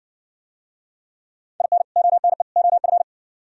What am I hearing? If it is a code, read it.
Morse code, heard as a keyed tone, about 56 wpm, 700 Hz. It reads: IN ONE OF